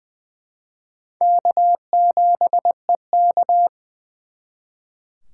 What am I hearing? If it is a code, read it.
Morse code, heard as a keyed tone, about 25 wpm, 700 Hz. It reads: K7EK
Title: Speed Racing - USA 1x2 Call Signs 20wpm